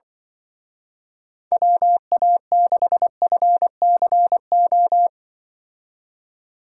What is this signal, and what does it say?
Morse code, heard as a keyed tone, about 24 wpm, 700 Hz. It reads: WA6FCO
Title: Speed Racing - USA 2x3 Call Signs 20wpm